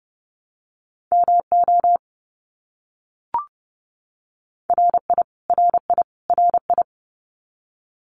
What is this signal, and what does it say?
Morse code, heard as a keyed tone, about 30 wpm, 700 Hz. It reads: MO RI RI RI
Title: US States Repeated 3x 30wpm